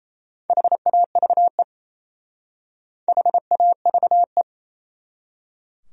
Morse code, heard as a keyed tone, about 37 wpm, 700 Hz. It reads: HAVE HAVE